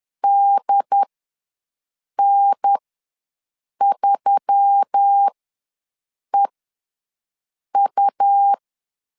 Morse code, heard as a keyed tone, about 10 wpm, spaced as for 6 wpm, 800 Hz. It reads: DN3EU
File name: LEZIONI / 5 LBVC / 5 L ws 10wpm 3636